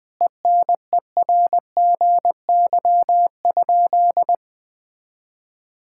Morse code, heard as a keyed tone, about 20 wpm, 700 Hz. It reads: ENERGY?
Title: Sentences from Top 1000 Words - Review Part 6 - 20wpm